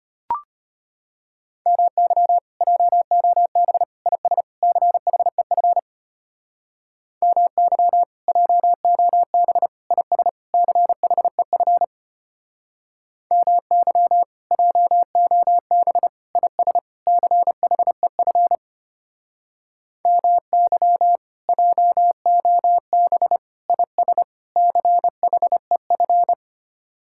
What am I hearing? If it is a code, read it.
Morse code, heard as a keyed tone, about 33 wpm, 700 Hz. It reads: MY JOB IS CHEF MY JOB IS CHEF MY JOB IS CHEF MY JOB IS CHEF